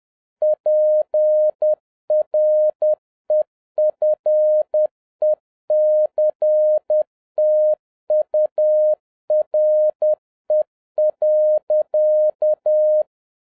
Morse code, heard as a keyed tone, about 10 wpm, 610 Hz. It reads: PREFECTURE.
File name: 2011-07-mp3 / n31 THE 3.11 2 10wpm